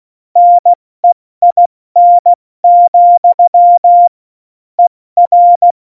Morse code, heard as a keyed tone, about 16 wpm, 700 Hz. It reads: NEIN, ER
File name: Morsekurs / Witze / Witz 24. November 2020, 70 F80